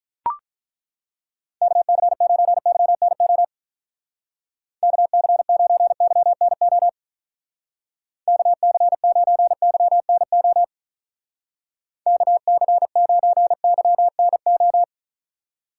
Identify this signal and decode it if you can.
Morse code, heard as a keyed tone, about 42 wpm, 700 Hz. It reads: KC9YNO KC9YNO KC9YNO KC9YNO
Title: Speed Racing - USA 2x3 Call Signs 35wpm